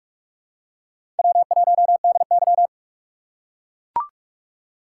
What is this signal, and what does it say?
Morse code, heard as a keyed tone, about 41 wpm, 700 Hz. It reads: W1DY
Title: USA Call Signs - 2x Character Spacing - 45wpm